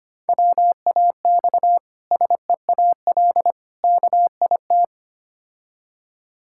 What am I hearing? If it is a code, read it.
Morse code, heard as a keyed tone, about 25 wpm, 700 Hz. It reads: WAX SEAL KIT
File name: practice / 25wpm / Difficult Words with Hints - 25wpm